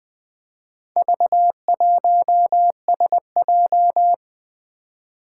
Morse code, heard as a keyed tone, about 20 wpm, 700 Hz. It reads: V1SJ
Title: Four Challenging Letter-Number Combinations 20wpm